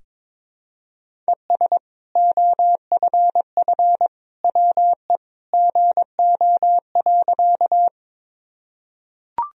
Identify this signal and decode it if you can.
Morse code, heard as a keyed tone, about 22 wpm, 700 Hz. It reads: ES OFF WE GO.
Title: Sentences from Top 200 Words and Abbreviations 22wpm